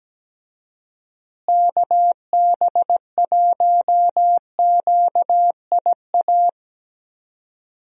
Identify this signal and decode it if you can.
Morse code, heard as a keyed tone, about 17 wpm, 700 Hz. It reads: KB1QIA
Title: USA Call Signs - 6 Characters - Repeated 3x - 17wpm